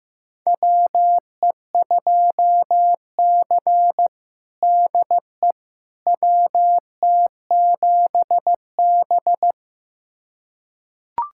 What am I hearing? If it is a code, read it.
Morse code, heard as a keyed tone, about 15 wpm, 700 Hz. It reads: WE2C DE WT7B
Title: USA Callsign DE Callsign - 4 Characters - 15wpm